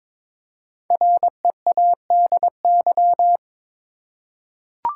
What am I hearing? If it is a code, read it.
Morse code, heard as a keyed tone, about 22 wpm, 690 Hz. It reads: READY
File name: 5 Letter Words - First 3 Letters Given - 22wpm